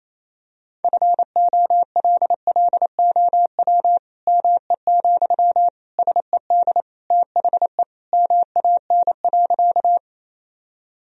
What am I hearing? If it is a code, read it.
Morse code, heard as a keyed tone, about 28 wpm, 700 Hz. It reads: FOLLOW ME, SED THE MAN.
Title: Sentences from Top 200 Words and Abbreviations 28wpm